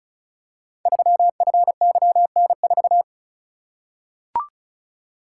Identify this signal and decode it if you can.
Morse code, heard as a keyed tone, about 36 wpm, 690 Hz. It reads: 3FYN4
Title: Code Groups - 5 Letter:Numbers - 35wpm